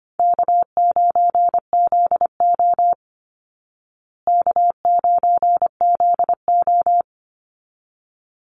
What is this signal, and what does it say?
Morse code, heard as a keyed tone, about 25 wpm, 700 Hz. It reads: K9ZO K9ZO
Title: Easy Call Signs Repeated 2x - 25wpm